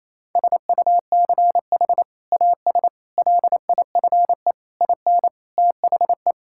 Morse code, heard as a keyed tone, about 28 wpm, 700 Hz. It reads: SUCH AS LIFE IN THE
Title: Sentences from Top 1000 Words - Review Part 5 - 28wpm